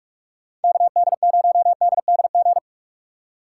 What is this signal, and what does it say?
Morse code, heard as a keyed tone, about 43 wpm, 700 Hz. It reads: KD0DDG